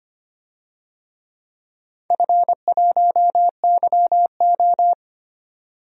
Morse code, heard as a keyed tone, about 25 wpm, 700 Hz. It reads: F1YO
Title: DXCC-100 Call Signs - 4 Characters - 25wpm